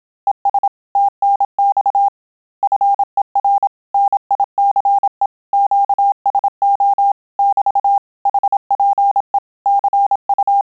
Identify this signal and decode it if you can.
Morse code, heard as a keyed tone, about 27 wpm, 800 Hz. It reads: ES TNX FER NICE QSO <BT> HPE CU
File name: Telegraphy-group / QSO CW 25 PPM